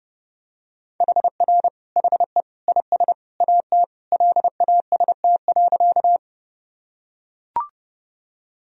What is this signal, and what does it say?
Morse code, heard as a keyed tone, about 30 wpm, 700 Hz. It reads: HR HE IS AT LAST.
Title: Sentences from Top 300 Words and Abbreviations 30wpm